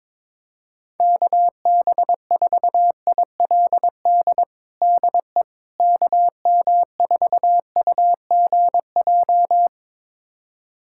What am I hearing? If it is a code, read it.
Morse code, heard as a keyed tone, about 22 wpm, 700 Hz. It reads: KB4ILD DE KM4UGJ
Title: USA Callsign DE Callsign - 6 Characters - 22wpm